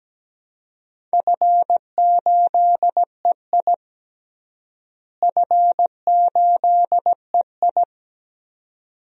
Morse code, heard as a keyed tone, about 17 wpm, 700 Hz. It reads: F8EI F8EI
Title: DXCC-100 Call Signs - 4 Characters - Repeated 2x - 17wpm